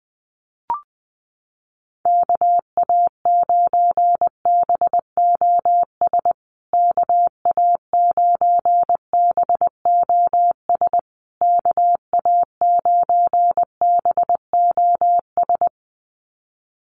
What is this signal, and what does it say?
Morse code, heard as a keyed tone, about 20 wpm, 700 Hz. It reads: KA9BOS KA9BOS KA9BOS
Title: USA Call Signs Repeated 3x 20wpm